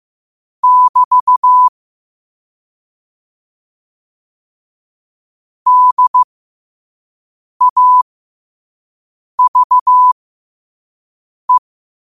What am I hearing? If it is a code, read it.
Morse code, heard as a keyed tone, about 15 wpm, spaced as for 5 wpm, 1000 Hz. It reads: <BT> DAVE
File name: qotdfiles / 5wpm